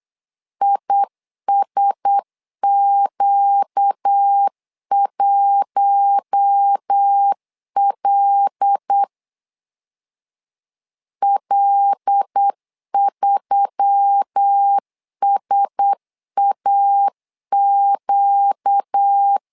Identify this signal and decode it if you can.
Morse code, heard as a keyed tone, about 8 wpm, 800 Hz. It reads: ISQ1L L3SAQ
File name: LEZIONI / 5 LBVC / 5 L 8wpm 2831